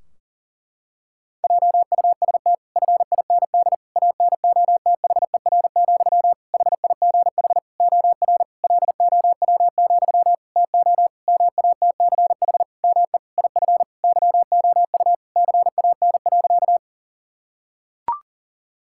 Morse code, heard as a keyed tone, about 37 wpm, 700 Hz. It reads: JUST FIND ANOTHER, HIGH OR LOW, TO MATCH ME IF YOU CAN.
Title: Sentences from Top 1600 Words 40wpm